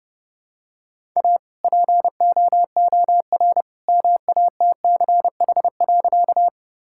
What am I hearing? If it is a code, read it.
Morse code, heard as a keyed tone, about 32 wpm, 700 Hz. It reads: A POOR MATCH.
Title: Sentences from Top 1600 Words 30wpm